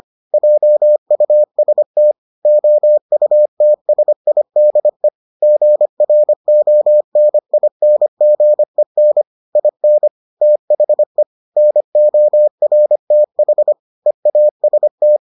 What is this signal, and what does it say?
Morse code, heard as a keyed tone, about 25 wpm, 600 Hz. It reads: JUST OUTSIDE GRONINGEN IN THE NORTH EAST